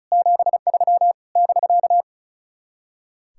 Morse code, heard as a keyed tone, about 44 wpm, 700 Hz. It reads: 73 <BK>
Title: Speed-Racing Common QSO Phrases - 35wpm